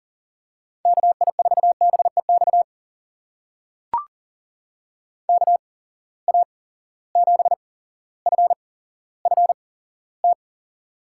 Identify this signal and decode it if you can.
Morse code, heard as a keyed tone, about 42 wpm, spaced as for 17 wpm, 700 Hz. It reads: KI4BEX KA7FFT
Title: USA Call Signs - 6 Characters - 8x Character Spacing - 40wpm